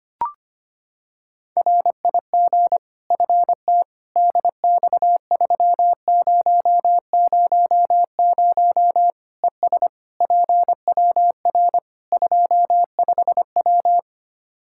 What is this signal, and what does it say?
Morse code, heard as a keyed tone, about 25 wpm, 700 Hz. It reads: RIG FT DX3000 ES PWR 25W
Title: Practice QSO - Exchange 2 - Info - 25wpm